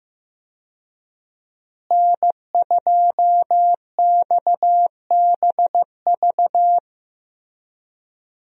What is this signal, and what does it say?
Morse code, heard as a keyed tone, about 15 wpm, 700 Hz. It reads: N2XBV
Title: USA Call Signs - 5 Characters - Repeated 3x - 15wpm